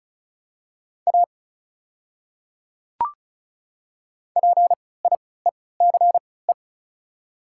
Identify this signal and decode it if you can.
Morse code, heard as a keyed tone, about 34 wpm, spaced as for 20 wpm, 700 Hz. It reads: A PIECE
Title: 3x Character Spacing - Top 500 Words - 35wpm